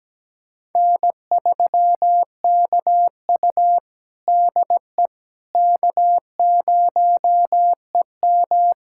Morse code, heard as a keyed tone, about 17 wpm, 700 Hz. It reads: N3KU DE K0EM
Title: USA Callsign DE Callsign - 4 Characters - 17wpm